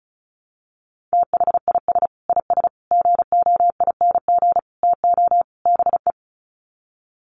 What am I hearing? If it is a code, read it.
Morse code, heard as a keyed tone, about 34 wpm, 700 Hz. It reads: THIS IS GOING TO BE
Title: Sets of 5 Words 35wpm